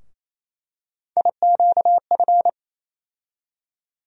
Morse code, heard as a keyed tone, about 28 wpm, 700 Hz. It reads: IQF
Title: USA Call Sign Postfixes - 3 Letters 28wpm